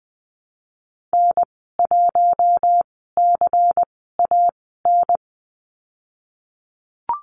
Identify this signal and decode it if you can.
Morse code, heard as a keyed tone, about 20 wpm, spaced as for 15 wpm, 700 Hz. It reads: N1CAN
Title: 122 Lesson 20g - Number 3 - Call Signs 20wpm